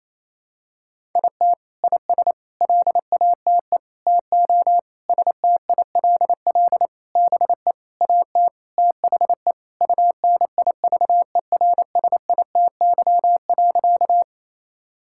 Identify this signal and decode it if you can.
Morse code, heard as a keyed tone, about 28 wpm, 700 Hz. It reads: IT IS LATE TO STILL BE AT THE UNIVERSITY.
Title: Sentences from Top 1000 Words - Review Part 5 - 28wpm